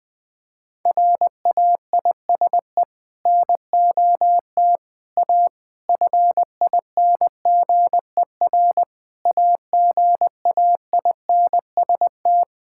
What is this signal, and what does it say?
Morse code, heard as a keyed tone, about 21 wpm, 700 Hz. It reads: RAISE NOT A FINGER AGAINST
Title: Sentences from Top 1600 Words 20wpm